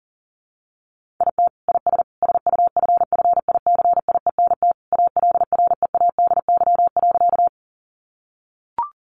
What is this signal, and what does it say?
Morse code, heard as a keyed tone, about 40 wpm, 700 Hz. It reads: IT IS SUFFICIENT ALREADY.
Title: Sentences from Top 1400 Words 40wpm